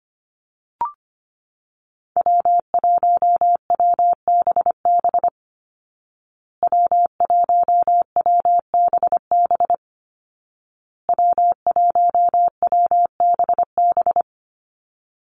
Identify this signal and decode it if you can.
Morse code, heard as a keyed tone, about 25 wpm, 700 Hz. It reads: W1WBB W1WBB W1WBB
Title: Call Signs Repeated 2x - 25wpm